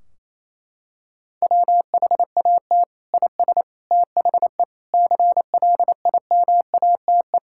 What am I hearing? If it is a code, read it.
Morse code, heard as a keyed tone, about 28 wpm, 700 Hz. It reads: WHAT IS THE CLIMATE
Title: Sentences from Top 1600 Words 28wpm